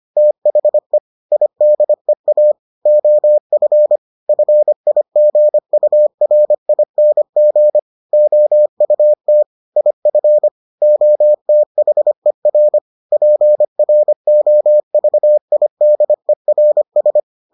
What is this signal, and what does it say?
Morse code, heard as a keyed tone, about 25 wpm, 600 Hz. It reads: THE IDEA OF FIGURING OUT IF OTHER PROVIDERS